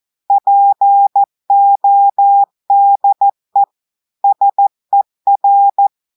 Morse code, heard as a keyed tone, about 14 wpm, 800 Hz. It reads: PODE SER
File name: audios / portugais / portugais texte en clair 2 vitesse 14 mots minutes